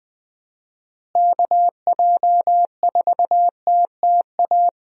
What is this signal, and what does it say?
Morse code, heard as a keyed tone, about 20 wpm, 700 Hz. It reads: KJ4TTA
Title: USA Call Signs - 6 Characters - Repeated 3x - 20wpm